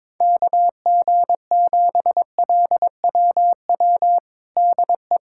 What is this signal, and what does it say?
Morse code, heard as a keyed tone, about 22 wpm, 700 Hz. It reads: KG7LWW DE
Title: USA Callsign DE Callsign 22wpm